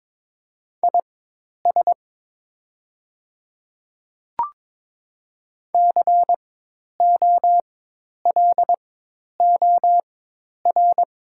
Morse code, heard as a keyed tone, about 22 wpm, spaced as for 10 wpm, 690 Hz. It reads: IS COLOR
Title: 4x Character Spacing - Top 300 Words - 22wpm